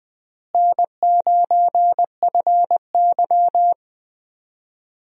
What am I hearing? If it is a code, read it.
Morse code, heard as a keyed tone, about 20 wpm, 700 Hz. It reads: N9FY
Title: USA Call Signs - 4 Characters - Repeated 2x - 20wpm